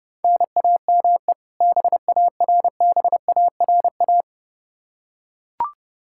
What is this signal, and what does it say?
Morse code, heard as a keyed tone, about 32 wpm, 700 Hz. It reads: NAME BARBARA
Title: Practice QSO - Exchange 1 - Beginning - 30wpm